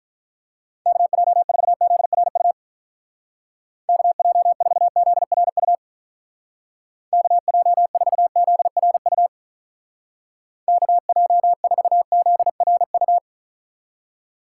Morse code, heard as a keyed tone, about 45 wpm, 700 Hz. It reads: KJ4ZRU KJ4ZRU KJ4ZRU KJ4ZRU